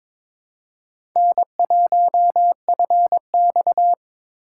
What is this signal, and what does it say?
Morse code, heard as a keyed tone, about 21 wpm, 700 Hz. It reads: N1FX
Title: USA Call Signs - 4 Characters - 4x Character Spacing - 22wpm